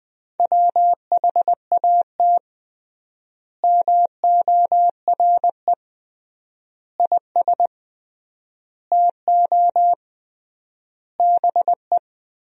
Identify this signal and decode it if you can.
Morse code, heard as a keyed tone, about 20 wpm, 700 Hz. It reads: WHAT MORE IS TO BE
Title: 3x Word-Spacing - Sentences from Top 500 Words - 20wpm